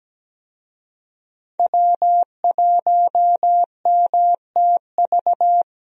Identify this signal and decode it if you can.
Morse code, heard as a keyed tone, about 17 wpm, 700 Hz. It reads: W1MTV
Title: USA Call Signs - 5 Characters - Repeated 3x - 17wpm